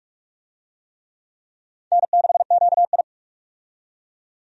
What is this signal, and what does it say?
Morse code, heard as a keyed tone, about 46 wpm, 700 Hz. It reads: N6QI
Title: USA Call Signs - 4 Characters - 45wpm